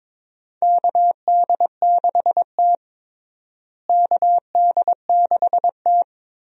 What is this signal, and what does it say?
Morse code, heard as a keyed tone, about 22 wpm, 700 Hz. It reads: KD6T KD6T
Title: USA Call Signs - 4 Characters - Repeated 2x - 22wpm